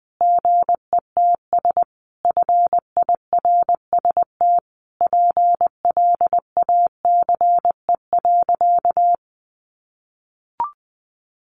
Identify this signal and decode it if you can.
Morse code, heard as a keyed tone, about 20 wpm, 700 Hz. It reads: GETS FIRST PLACE.
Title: Additional Easy Phrases 20wpm